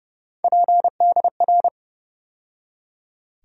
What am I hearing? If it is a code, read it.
Morse code, heard as a keyed tone, about 32 wpm, 700 Hz. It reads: PDR
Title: USA Call Sign Postfixes - 3 Letters 30wpm